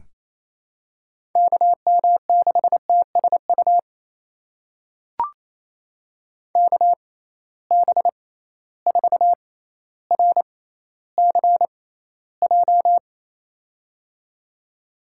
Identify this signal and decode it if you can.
Morse code, heard as a keyed tone, about 28 wpm, spaced as for 14 wpm, 700 Hz. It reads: KM6TSU KB4RCJ
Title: USA Call Signs - 6 Characters - 6x Character Spacing - 28wpm